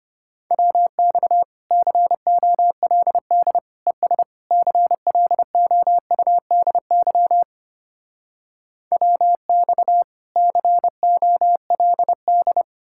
Morse code, heard as a keyed tone, about 33 wpm, 700 Hz. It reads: WX COLD ES CLOUDY WX COLD